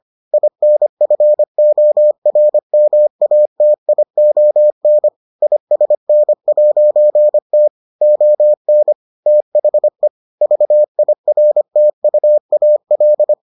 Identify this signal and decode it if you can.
Morse code, heard as a keyed tone, about 25 wpm, 600 Hz. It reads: INFORMATION ISN'T ON THE VIRTUAL